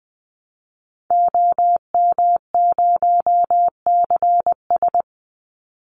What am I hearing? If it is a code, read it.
Morse code, heard as a keyed tone, about 20 wpm, 700 Hz. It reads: OM0CS
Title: Call Signs Repeated 3x - 20wpm